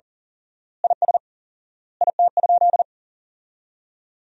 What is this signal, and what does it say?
Morse code, heard as a keyed tone, about 40 wpm, 700 Hz. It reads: IS IT?
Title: 4x Word-Spacing - Sentences from Top 100 Words - 40wpm